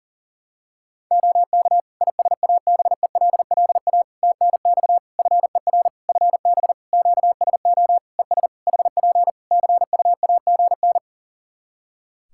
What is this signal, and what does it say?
Morse code, heard as a keyed tone, about 37 wpm, 700 Hz. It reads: OK ISABELLA TNX FER FB QSO ES HP CUAGN
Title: Practice QSO - Exchange 4 - Ending - 40wpm